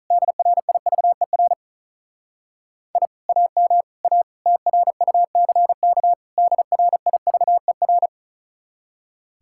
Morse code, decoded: DRIVER I AM A TRUCK DRIVER

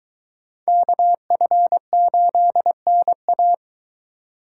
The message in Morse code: KF8NA

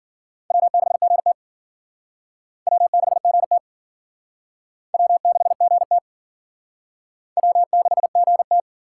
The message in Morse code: W6GT W6GT W6GT W6GT

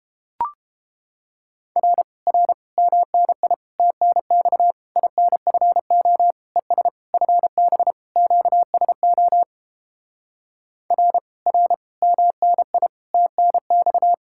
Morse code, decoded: R R MNI TNX INFO ES FB QSO R R MNI TNX

